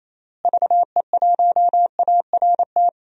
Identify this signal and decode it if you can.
Morse code, heard as a keyed tone, about 28 wpm, 700 Hz. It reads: VE1ART
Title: DXCC-100 Call Signs - 6 Characters - Repeated 3x - 28wpm